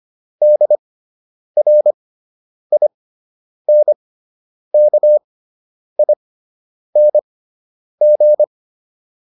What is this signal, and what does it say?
Morse code, heard as a keyed tone, about 25 wpm, spaced as for 9 wpm, 600 Hz. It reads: DRINKING